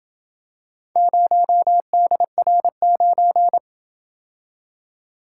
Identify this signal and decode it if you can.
Morse code, heard as a keyed tone, about 27 wpm, 700 Hz. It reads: 0DR9